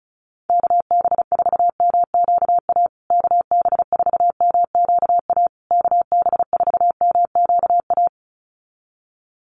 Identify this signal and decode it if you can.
Morse code, heard as a keyed tone, about 34 wpm, 700 Hz. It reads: KB4MQA KB4MQA KB4MQA
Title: USA Call Signs Repeated 3x 35wpm